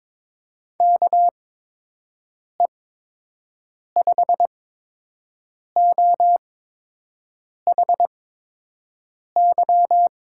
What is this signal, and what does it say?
Morse code, heard as a keyed tone, about 22 wpm, spaced as for 6 wpm, 700 Hz. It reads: KE5OHY
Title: USA Call Signs - 6 Characters - 8x Character Spacing - 22wpm